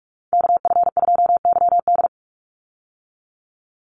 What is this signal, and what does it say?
Morse code, heard as a keyed tone, about 44 wpm, 700 Hz. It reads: KF2YD
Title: USA Call Signs 45wpm